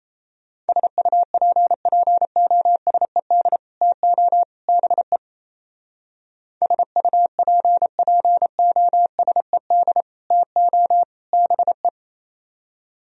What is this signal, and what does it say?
Morse code, heard as a keyed tone, about 32 wpm, 700 Hz. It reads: SUPPOSED TO BE SUPPOSED TO BE
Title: Speed-Racing - Sets of 3 Words 28wpm